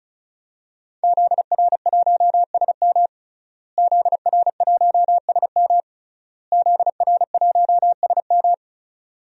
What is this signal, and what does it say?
Morse code, heard as a keyed tone, about 34 wpm, 690 Hz. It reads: ZR1SM ZR1SM ZR1SM
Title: DXCC-100 Call Signs - 5 Characters - Repeated 3x - 35wpm